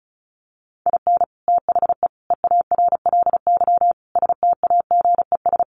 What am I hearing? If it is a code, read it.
Morse code, heard as a keyed tone, about 34 wpm, 700 Hz. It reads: IN THE EARLY STAGES
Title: Sets of 5 Words Encore 35wpm